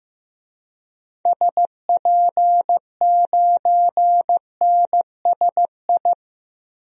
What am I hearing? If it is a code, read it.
Morse code, heard as a keyed tone, about 15 wpm, 700 Hz. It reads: SP9NSI
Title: DXCC-100 Call Signs - 6 Characters - Repeated 2x - 15wpm